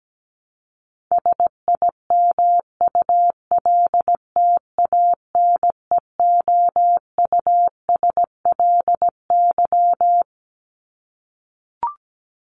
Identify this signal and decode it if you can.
Morse code, heard as a keyed tone, about 17 wpm, 700 Hz. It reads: SIMULTANEOUSLY
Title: LY Suffix - Top 100 Words with Base Words 17wpm